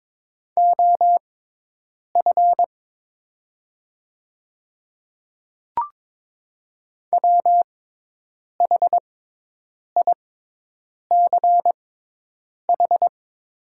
Morse code, heard as a keyed tone, about 22 wpm, spaced as for 8 wpm, 700 Hz. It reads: OF WHICH
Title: 6x Character Spacing - Top 500 Words - 22wpm